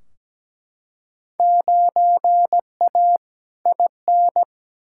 Morse code, heard as a keyed tone, about 17 wpm, 700 Hz. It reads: 9A IN
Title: Field Day Exchanges Encore - 17wpm